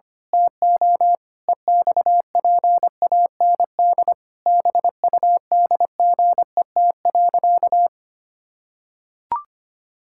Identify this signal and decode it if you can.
Morse code, heard as a keyed tone, about 25 wpm, 700 Hz. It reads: TO EXPAND BUDGET.
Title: Pangram Challenge v2 - 25wpm